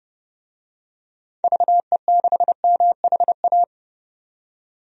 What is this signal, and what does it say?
Morse code, heard as a keyed tone, about 30 wpm, 700 Hz. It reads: VE6MHA
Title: DXCC-100 Call Signs - 6 Characters - 30wpm